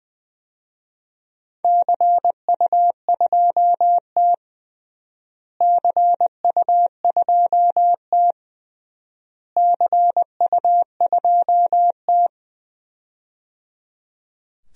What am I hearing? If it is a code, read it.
Morse code, heard as a keyed tone, about 20 wpm, 700 Hz. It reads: CU2T CU2T CU2T
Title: DXCC-100 Call Signs - 4 Characters - Repeated 3x - 20wpm